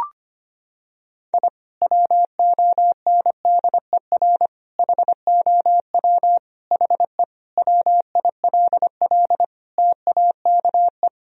Morse code, heard as a keyed tone, about 25 wpm, 700 Hz. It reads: I WONDER HOW HE WILL TAKE
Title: Sentences from Top 1000 Words - Review Part 3 - 25wpm